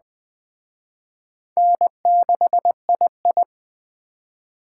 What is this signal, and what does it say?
Morse code, heard as a keyed tone, about 20 wpm, 700 Hz. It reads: N6II
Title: Speed Racing - USA 1x2 Call Signs 17wpm